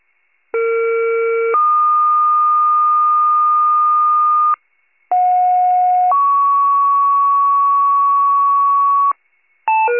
Two-tone page, heard as a one-second tone, then a three-second tone, about 460 then 1220 Hz, about 730 then 1120 Hz. It is heard from 0.5 to 9.1 s.